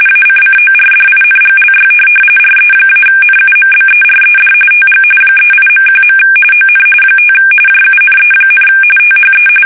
RTTY, about 45.45 baud. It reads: ACCORDINGLY RE-NUMBERED TG 38.1. ON 5 JUNE, THE
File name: fsk850 / USS-Massachusetts